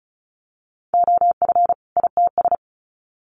Morse code, heard as a keyed tone, about 34 wpm, 700 Hz. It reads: OF ITS
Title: Sets of 2 Words 35wpm